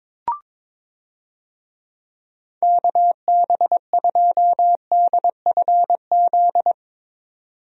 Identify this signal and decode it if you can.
Morse code, heard as a keyed tone, about 22 wpm, 700 Hz. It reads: KB2DFZ
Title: USA Call Signs - 6 Characters - 22wpm